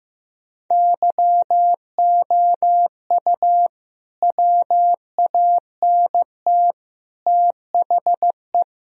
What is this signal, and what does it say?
Morse code, heard as a keyed tone, about 15 wpm, 700 Hz. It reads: YOU WANT THE
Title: Sentences from Top 1000 Words - Review Part 4 - 15wpm